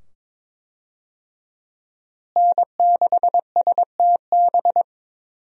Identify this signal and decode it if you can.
Morse code, heard as a keyed tone, about 22 wpm, 700 Hz. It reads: N6STB
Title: USA Call Signs - 5 Characters - Repeated 2x - 22wpm